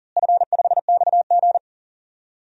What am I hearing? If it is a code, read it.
Morse code, heard as a keyed tone, about 40 wpm, 700 Hz. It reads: F5XG
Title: DXCC-100 Call Signs - 4 Characters - Repeated 2x - 40wpm